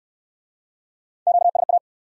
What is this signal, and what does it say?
Morse code, heard as a keyed tone, about 68 wpm, 700 Hz. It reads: QSA